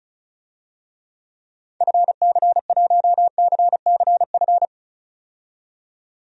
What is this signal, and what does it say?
Morse code, heard as a keyed tone, about 34 wpm, 700 Hz. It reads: FC1CCF
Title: DXCC-100 Call Signs - 6 Characters - 35wpm